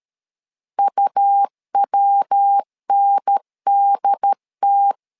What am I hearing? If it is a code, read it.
Morse code, heard as a keyed tone, about 13 wpm, 800 Hz. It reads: UWNDT
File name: LEZIONI / 3 MWRO / 3 MW 12wpm 1633